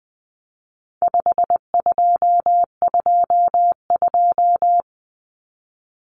Morse code, heard as a keyed tone, about 20 wpm, 700 Hz. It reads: 5222
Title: Numbers - 4-Digit 20wpm